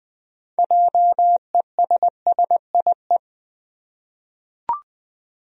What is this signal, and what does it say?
Morse code, heard as a keyed tone, about 20 wpm, 700 Hz. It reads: JESSIE
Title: USA Common Names - 20wpm